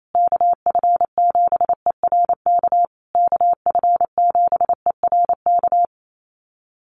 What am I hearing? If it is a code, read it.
Morse code, heard as a keyed tone, about 28 wpm, 700 Hz. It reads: KF7ERK KF7ERK
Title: USA Call Signs Repeated 3x 28wpm